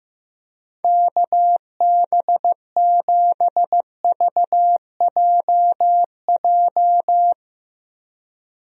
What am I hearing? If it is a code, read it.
Morse code, heard as a keyed tone, about 15 wpm, 700 Hz. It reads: KB7VJJ
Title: USA Call Signs - 6 Characters - Repeated 3x - 15wpm